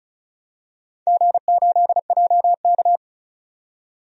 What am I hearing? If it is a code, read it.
Morse code, heard as a keyed tone, about 36 wpm, 700 Hz. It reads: G8JK
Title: DXCC-100 Call Signs - 4 Characters - 35wpm